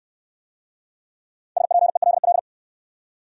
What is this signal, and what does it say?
Morse code, heard as a keyed tone, about 66 wpm, 700 Hz. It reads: SPELL